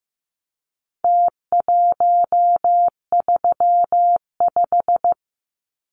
Morse code, heard as a keyed tone, about 15 wpm, 700 Hz. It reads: T135
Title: Numbers - Contest Style - 4-Digit 15wpm